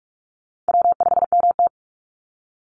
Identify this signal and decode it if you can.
Morse code, heard as a keyed tone, about 44 wpm, 700 Hz. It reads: W5MT
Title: Easy Call Signs Repeated 3x - 45wpm